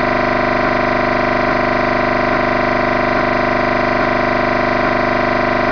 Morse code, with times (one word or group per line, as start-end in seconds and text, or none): none